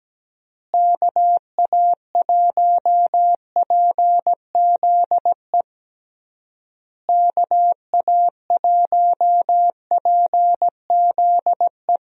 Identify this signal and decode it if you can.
Morse code, heard as a keyed tone, about 17 wpm, 700 Hz. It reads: KA1PZE KA1PZE
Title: USA Call Signs - 6 Characters - Repeated 3x - 17wpm